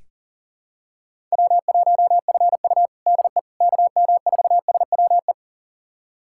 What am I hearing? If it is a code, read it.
Morse code, heard as a keyed tone, about 37 wpm, 700 Hz. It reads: W1FU DE KM4SWE
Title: USA Callsign DE Callsign 40wpm